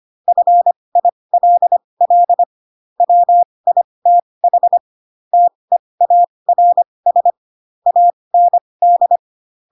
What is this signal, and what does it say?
Morse code, heard as a keyed tone, about 25 wpm, spaced as for 20 wpm, 700 Hz. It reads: FILL WITH TEARS AND
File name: Pinocchio / Pinocchio ch8 15 25 ik1hge